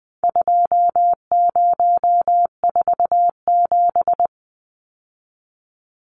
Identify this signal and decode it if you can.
Morse code, heard as a keyed tone, about 20 wpm, 700 Hz. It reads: 2047
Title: Numbers - 4-Digit 20wpm